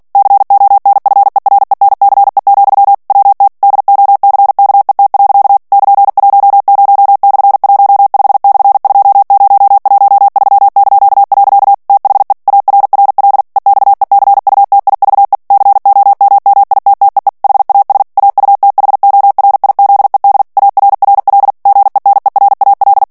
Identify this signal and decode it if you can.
Morse code, heard as a keyed tone, about 48 wpm, 800 Hz. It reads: CONFERENCE, WT DOCKET. <KN>10/15/2013). THE ARRL EXECUTIVE COMMITTEE HAS AUTHORIZED ARRL GENERAL
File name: ANGLAIS texte clair 3 vitesse 45 mots minute